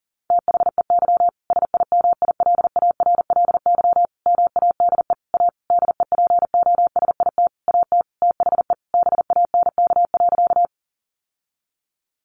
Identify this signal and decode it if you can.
Morse code, heard as a keyed tone, about 40 wpm, 700 Hz. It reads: THEY SIMILARLY MADE A DEPOSIT AT THE BANK.
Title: Sentences from Top 1400 Words 40wpm